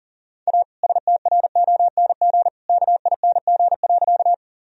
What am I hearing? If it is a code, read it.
Morse code, heard as a keyed tone, about 37 wpm, 700 Hz. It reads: A STRONG KING.
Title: Sentences from Top 1000 Words - Review Part 3 - 40wpm